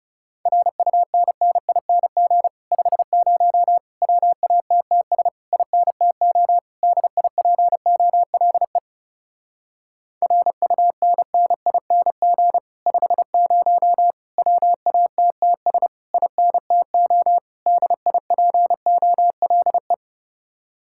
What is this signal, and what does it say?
Morse code, heard as a keyed tone, about 34 wpm, 700 Hz. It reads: RUNNING 50 WATTS INTO DIPOLE RUNNING 50 WATTS INTO DIPOLE
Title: Speed-Racing Common QSO Phrases Encore - 30wpm